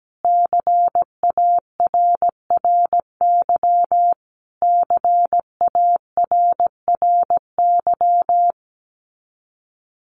Morse code, heard as a keyed tone, about 17 wpm, 700 Hz. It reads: CARRY CARRY
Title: Top 500 Words Repeated 17wpm